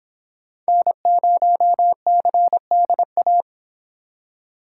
Morse code, heard as a keyed tone, about 26 wpm, 700 Hz. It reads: N0CDA